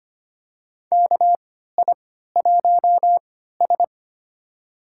Morse code, heard as a keyed tone, about 25 wpm, spaced as for 14 wpm, 700 Hz. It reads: KI1S
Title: USA Call Signs - 3x Character Spacing - 25wpm